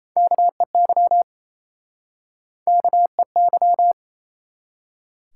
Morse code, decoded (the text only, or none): KEY KEY